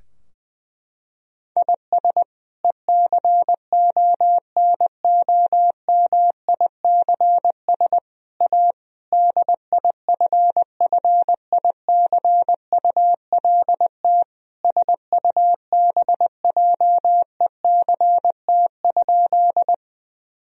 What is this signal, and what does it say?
Morse code, heard as a keyed tone, about 21 wpm, 700 Hz. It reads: IS ECONOMICS A DIFFICULT SUBJECT?
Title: Sentences from Top 1000 Words - Review Part 3 - 20wpm